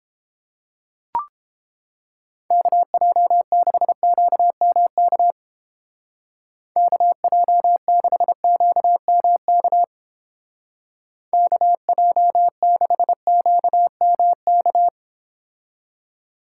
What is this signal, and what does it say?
Morse code, heard as a keyed tone, about 29 wpm, 700 Hz. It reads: KJ6QMK KJ6QMK KJ6QMK